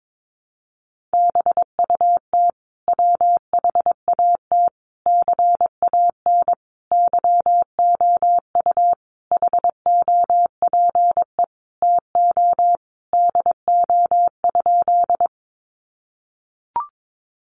Morse code, heard as a keyed tone, about 22 wpm, 700 Hz. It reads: BUT WHAT CAN YOU HOPE TO DO?
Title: Sentences from Top 1000 Words 22wpm